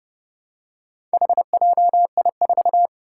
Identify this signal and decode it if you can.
Morse code, heard as a keyed tone, about 30 wpm, 700 Hz. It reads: HJI4
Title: Four Challenging Letter-Number Combinations 30wpm